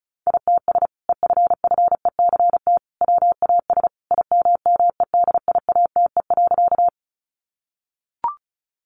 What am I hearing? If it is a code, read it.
Morse code, heard as a keyed tone, about 34 wpm, 700 Hz. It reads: ITS EFFECT WAS IMMEDIATE.
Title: Sentences from Top 1200 Words 35wpm